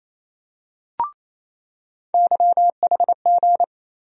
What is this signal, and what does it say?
Morse code, heard as a keyed tone, about 27 wpm, 700 Hz. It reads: YHG
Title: USA Call Sign Postfixes - 3 Letters 28wpm